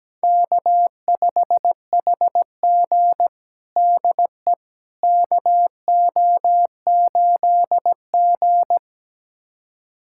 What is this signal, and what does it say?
Morse code, heard as a keyed tone, about 17 wpm, 700 Hz. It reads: K5HG DE KO8G
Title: USA Callsign DE Callsign - 4 Characters - 17wpm